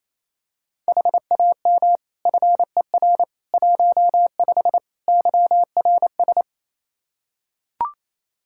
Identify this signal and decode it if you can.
Morse code, heard as a keyed tone, about 28 wpm, 700 Hz. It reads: HAM FER 15 YRS
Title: Practice QSO - Exchange 3 - More Info - 28wpm